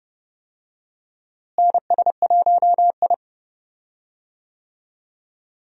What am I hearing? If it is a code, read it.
Morse code, heard as a keyed tone, about 30 wpm, 700 Hz. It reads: NS1I
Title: USA Call Signs - 4 Characters - Repeated 3x - 30wpm